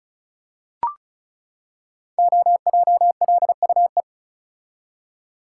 Morse code, heard as a keyed tone, about 36 wpm, 690 Hz. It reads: OJLUE
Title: Code Groups - 5 Letter:Numbers - 35wpm